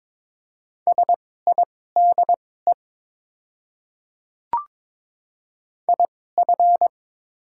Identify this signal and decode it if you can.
Morse code, heard as a keyed tone, about 22 wpm, spaced as for 16 wpm, 700 Hz. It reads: SIDE IF
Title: 2x Character Spacing - Top 100 Words - 22wpm